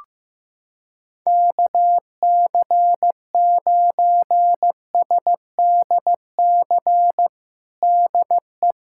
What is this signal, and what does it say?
Morse code, heard as a keyed tone, about 16 wpm, 700 Hz. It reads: KC9SDC DE
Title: USA Callsign DE Callsign - 6 Characters - 15wpm